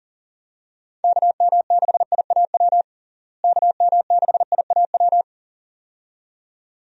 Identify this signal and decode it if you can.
Morse code, heard as a keyed tone, about 40 wpm, 700 Hz. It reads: KM6IAW KM6IAW
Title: USA Call Signs - 6 Characters - Repeated 2x - 40wpm